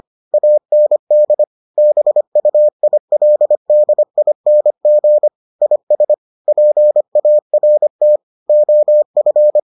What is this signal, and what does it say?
Morse code, heard as a keyed tone, about 25 wpm, 600 Hz. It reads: AND BUILDING IS PART OF